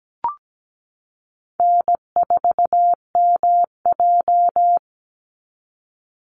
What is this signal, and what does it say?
Morse code, heard as a keyed tone, about 17 wpm, 700 Hz. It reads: N4MJ
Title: NA SOTA Chaser Callsigns 17wpm